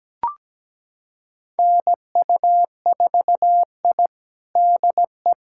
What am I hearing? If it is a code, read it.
Morse code, heard as a keyed tone, about 17 wpm, 700 Hz. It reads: NU4I DE
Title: USA Callsign DE Callsign - 4 Characters - 17wpm